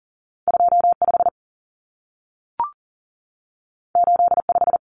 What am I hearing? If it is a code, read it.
Morse code, heard as a keyed tone, about 40 wpm, 700 Hz. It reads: 25 85
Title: Numbers - 2-Digit 40wpm